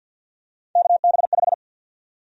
Morse code, heard as a keyed tone, about 53 wpm, 700 Hz. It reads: KB5
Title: USA Call Sign Prefixes - 50wpm